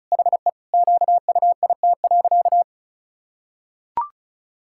Morse code, heard as a keyed tone, about 36 wpm, 700 Hz. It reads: HE QUIT.
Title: Pangram Challenge v2 - 35wpm